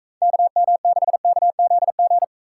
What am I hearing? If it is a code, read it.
Morse code, heard as a keyed tone, about 42 wpm, 700 Hz. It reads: KM6KZG